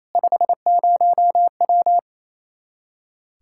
Morse code, heard as a keyed tone, about 27 wpm, 700 Hz. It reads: 50W
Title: K1USN SST QSO Practice 28wpm